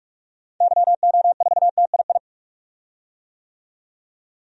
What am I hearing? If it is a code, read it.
Morse code, heard as a keyed tone, about 45 wpm, 700 Hz. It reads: YO4TII
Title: DXCC-100 Call Signs - 6 Characters - 45wpm